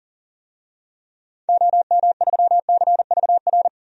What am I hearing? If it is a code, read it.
Morse code, heard as a keyed tone, about 40 wpm, 700 Hz. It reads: OM3CVR
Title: DXCC-100 Call Signs - 6 Characters - 40wpm